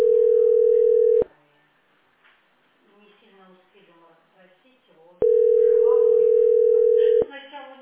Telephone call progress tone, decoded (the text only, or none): ringback tone